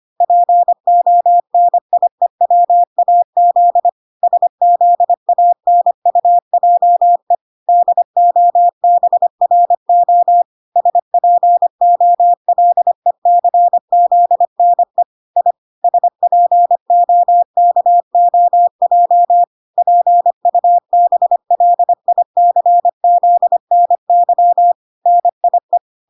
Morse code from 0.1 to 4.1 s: PONIEWAZ